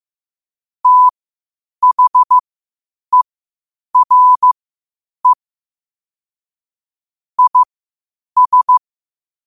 Morse code, heard as a keyed tone, about 15 wpm, spaced as for 8 wpm, 1000 Hz. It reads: THERE IS
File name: qotdfiles / 8wpm